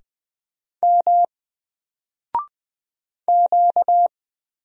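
Morse code, heard as a keyed tone, about 20 wpm, 700 Hz. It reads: M Q
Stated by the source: Single letter and number Morse drill at 20 wpm, from N4LCD.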